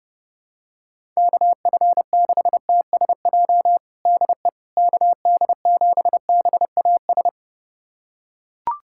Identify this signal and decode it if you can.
Morse code, heard as a keyed tone, about 30 wpm, 700 Hz. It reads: KF6TSJ DE KD7BAS
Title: USA Callsign DE Callsign - 6 Characters - 30wpm